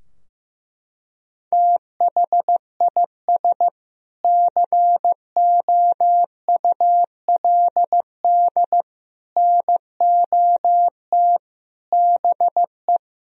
Morse code, heard as a keyed tone, about 16 wpm, 700 Hz. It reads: THIS COULD NOT BE
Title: Sentences from Top 1000 Words - Review Part 3 - 15wpm